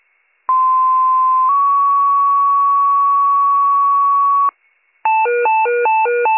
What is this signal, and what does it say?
Two-tone page, heard as a one-second tone, then a three-second tone, about 1040 then 1120 Hz. It is heard from 0.5 to 4.5 s.